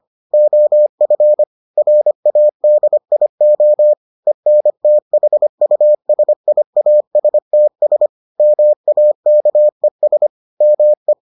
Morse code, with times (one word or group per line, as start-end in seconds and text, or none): 0.3-1.4 OF
1.8-3.9 RADIO
4.3-8.1 ENTHUSIASTS
8.4-10.3 MAKES
10.6-11.1 ME